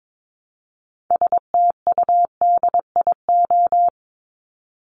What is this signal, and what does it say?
Morse code, heard as a keyed tone, about 22 wpm, 700 Hz. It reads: STUDIO
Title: Top 1001-1250 Words 22wpm